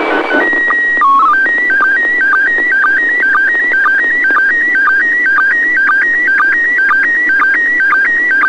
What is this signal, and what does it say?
SSTV (slow-scan television), PD 120.
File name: ISS 23h00 UTC 145MHz800 pic03